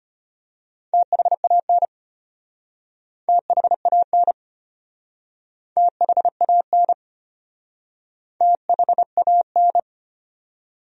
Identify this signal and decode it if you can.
Morse code, heard as a keyed tone, about 34 wpm, 700 Hz. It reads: THAN THAN THAN THAN